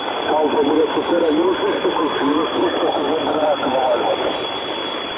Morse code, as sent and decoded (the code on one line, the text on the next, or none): none